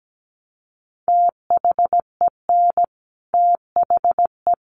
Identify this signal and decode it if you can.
Morse code, heard as a keyed tone, about 17 wpm, 700 Hz. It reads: THEN THE
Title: Sentences from Top 1500 Words 17wpm